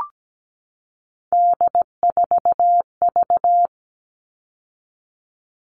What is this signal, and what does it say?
Morse code, heard as a keyed tone, about 17 wpm, 700 Hz. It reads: D4V
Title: All Cut Numbers - 3-Digit 17wpm